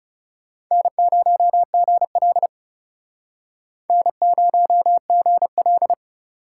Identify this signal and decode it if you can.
Morse code, heard as a keyed tone, about 40 wpm, 700 Hz. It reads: N0GL N0GL